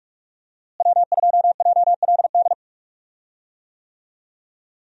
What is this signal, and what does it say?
Morse code, heard as a keyed tone, about 44 wpm, 700 Hz. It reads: W2JLD
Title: USA Call Signs - 5 Characters - Repeated 2x - 45wpm